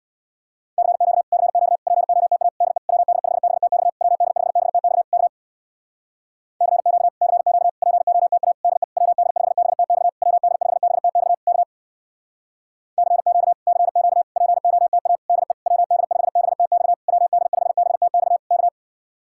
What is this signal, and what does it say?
Morse code, heard as a keyed tone, about 66 wpm, 700 Hz. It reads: CQ CQ POTA DE WD5BTX WD5BTX K CQ CQ POTA DE WD5BTX WD5BTX K CQ CQ POTA DE WD5BTX WD5BTX K